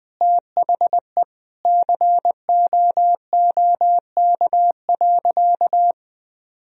Morse code, heard as a keyed tone, about 21 wpm, 700 Hz. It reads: THE COOK.
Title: Sentences from Top 1000 Words - Review Part 4 - 20wpm